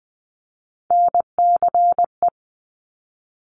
Morse code, heard as a keyed tone, about 20 wpm, 700 Hz. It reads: NCE
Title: Top Three Letter Combinations in Words 20wpm